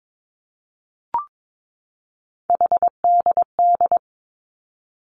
Morse code, heard as a keyed tone, about 22 wpm, 700 Hz. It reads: HDD
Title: Three Challenging Letter-Number Combinations 22wpm